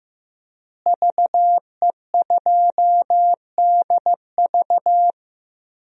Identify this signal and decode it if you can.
Morse code, heard as a keyed tone, about 16 wpm, 700 Hz. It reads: VE2DV
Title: DXCC-100 Call Signs - 5 Characters - Repeated 2x - 15wpm